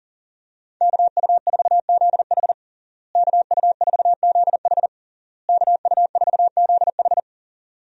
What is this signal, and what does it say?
Morse code, heard as a keyed tone, about 40 wpm, 700 Hz. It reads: KU4ZH KU4ZH KU4ZH
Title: USA Call Signs - 5 Characters - Repeated 3x - 40wpm